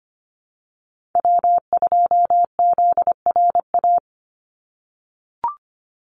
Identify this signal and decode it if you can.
Morse code, heard as a keyed tone, about 25 wpm, 700 Hz. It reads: W2ZRA
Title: Call Signs - 25wpm